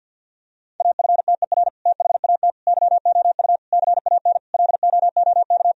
Morse code, heard as a keyed tone, about 50 wpm, 700 Hz. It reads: AFTER THAT YOU CAN LOOK
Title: Sentences from Top 1000 Words - Review Part 2 - 50wpm